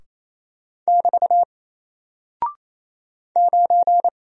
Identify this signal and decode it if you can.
Morse code, heard as a keyed tone, about 27 wpm, 700 Hz. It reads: <BT> 9
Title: Single Character (Letter:Number:Punctuation:Prosign) 28wpm